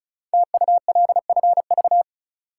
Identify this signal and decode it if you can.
Morse code, heard as a keyed tone, about 36 wpm, 700 Hz. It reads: TULFV
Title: Code Groups - 5 Letters - 35wpm